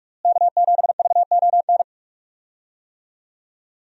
Morse code, heard as a keyed tone, about 45 wpm, 700 Hz. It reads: K7VON
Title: USA Call Signs - 5 Characters - Repeated 2x - 45wpm